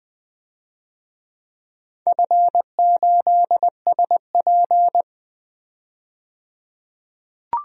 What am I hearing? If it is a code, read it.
Morse code, heard as a keyed tone, about 20 wpm, 700 Hz. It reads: F8SP
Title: DXCC-100 Call Signs - 4 Characters - 20wpm